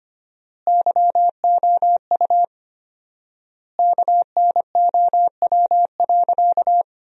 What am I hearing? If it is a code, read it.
Morse code, heard as a keyed tone, about 25 wpm, 700 Hz. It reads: YOU KNOW.
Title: 4x Word-Spacing - Sentences from Top 100 Words - 25wpm